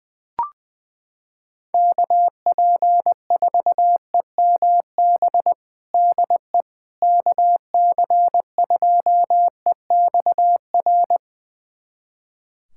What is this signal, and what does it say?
Morse code, heard as a keyed tone, about 20 wpm, 700 Hz. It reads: KP4EMB DE KC2EXR
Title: USA Callsign DE Callsign 20wpm